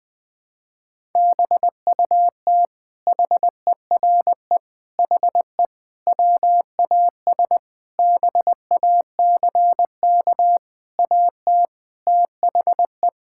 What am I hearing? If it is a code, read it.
Morse code, heard as a keyed tone, about 19 wpm, 700 Hz. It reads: BUT HERE HE WAS BACK AT THE
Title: Sentences from Top 1600 Words 20wpm